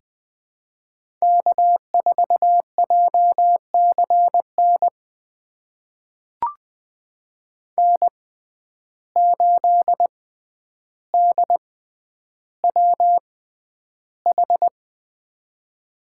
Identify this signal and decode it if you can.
Morse code, heard as a keyed tone, about 20 wpm, spaced as for 10 wpm, 700 Hz. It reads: K4JCN N8DWH
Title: USA Call Signs - 5 Characters - 6x Character Spacing - 20wpm